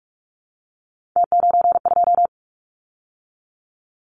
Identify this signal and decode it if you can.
Morse code, heard as a keyed tone, about 46 wpm, 690 Hz. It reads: T92
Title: Callsign Prefixes for Top 50 DXCC - 45wpm